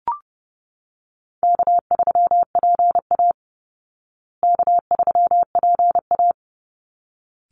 Morse code, heard as a keyed tone, about 30 wpm, 700 Hz. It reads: K3PA K3PA
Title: Easy Call Signs Repeated 2x - 30wpm